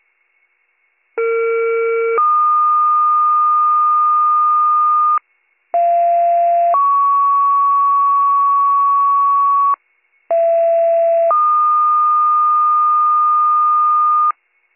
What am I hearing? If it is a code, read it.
Two-tone page, heard as a one-second tone, then a three-second tone, about 460 then 1220 Hz, about 690 then 1120 Hz, about 670 then 1220 Hz. It is heard from 1.2 to 14.3 s.